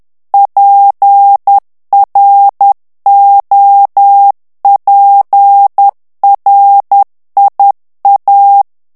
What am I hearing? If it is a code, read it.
Morse code, heard as a keyed tone, about 10 wpm, 800 Hz. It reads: PROPRIA
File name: ITALIEN texte clair 2 vitesse 10 mots minute